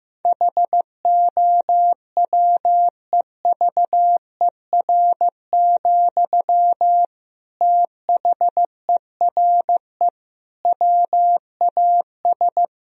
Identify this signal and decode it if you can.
Morse code, heard as a keyed tone, about 15 wpm, 700 Hz. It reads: HOWEVER, THERE WAS
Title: Sentences from Top 1000 Words - Review Part 4 - 15wpm